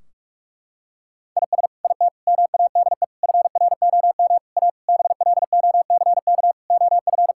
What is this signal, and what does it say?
Morse code, heard as a keyed tone, about 44 wpm, 700 Hz. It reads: IS IT MADE FROM A BLOCK OF
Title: Sentences from Top 1600 Words 45wpm